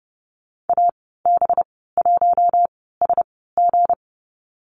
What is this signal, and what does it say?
Morse code, heard as a keyed tone, about 30 wpm, spaced as for 17 wpm, 700 Hz. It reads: AB1SG
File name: Lesson 37g - Letter x - Call Signs 30wpm